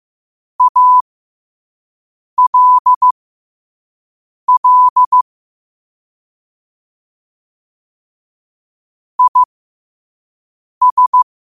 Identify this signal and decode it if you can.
Morse code, heard as a keyed tone, about 15 wpm, spaced as for 5 wpm, 1000 Hz. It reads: ALL IS
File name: qotdfiles / 5wpm